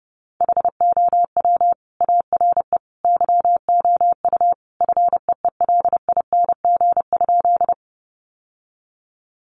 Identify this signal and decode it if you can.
Morse code, heard as a keyed tone, about 30 wpm, 700 Hz. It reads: HOW ARE YOU FEELING?
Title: Sentences from Top 1200 Words 30wpm